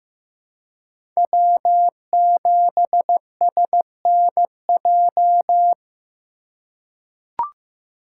Morse code, heard as a keyed tone, about 15 wpm, 700 Hz. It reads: W7SNJ
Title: USA Call Signs - 5 Characters - 2x Character Spacing - 15wpm